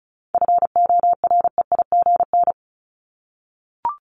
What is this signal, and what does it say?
Morse code, heard as a keyed tone, about 34 wpm, 700 Hz. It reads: FOREIGN
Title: Top 1251-1500 Words 35wpm